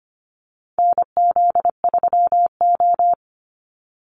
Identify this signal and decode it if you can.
Morse code, heard as a keyed tone, about 25 wpm, 700 Hz. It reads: NZ3O
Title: Call Signs - 25wpm